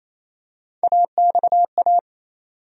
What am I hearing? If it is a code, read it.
Morse code, heard as a keyed tone, about 28 wpm, 700 Hz. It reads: AXA
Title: USA Call Sign Postfixes - 3 Letters 28wpm